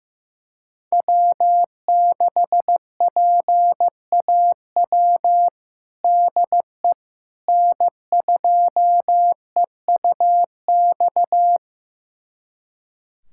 Morse code, heard as a keyed tone, about 15 wpm, 700 Hz. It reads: W6PAW DE N2EUX
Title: USA Callsign DE Callsign - 5 Characters - 15wpm